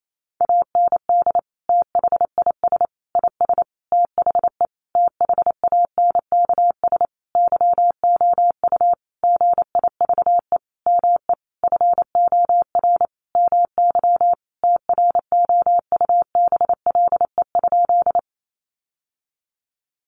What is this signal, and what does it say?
Morse code, heard as a keyed tone, about 28 wpm, 700 Hz. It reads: AND THIS IS THE THANKS YOU GIVE ME FOR MY TROUBLE?
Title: Sentences from Top 1200 Words 28wpm